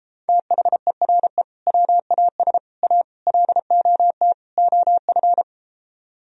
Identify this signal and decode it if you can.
Morse code, heard as a keyed tone, about 33 wpm, 700 Hz. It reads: THERE WAS A LOT OF